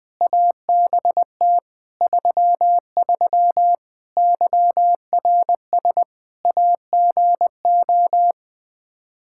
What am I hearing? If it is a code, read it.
Morse code, heard as a keyed tone, about 21 wpm, 700 Hz. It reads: ABT 33 YRS AGO
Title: Practice QSO - Exchange 3 - More Info - 20wpm